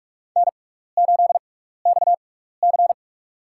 Morse code, heard as a keyed tone, about 44 wpm, spaced as for 15 wpm, 700 Hz. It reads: N8XC
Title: USA Call Signs - 4 Characters - 6x Character Spacing - 45wpm